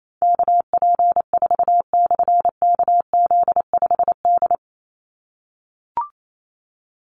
Morse code, heard as a keyed tone, about 28 wpm, 700 Hz. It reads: KP4/KZ5D
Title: Call Signs - 28wpm